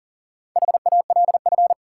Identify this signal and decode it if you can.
Morse code, heard as a keyed tone, about 40 wpm, 700 Hz. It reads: HALF
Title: Four Letter Words - 40wpm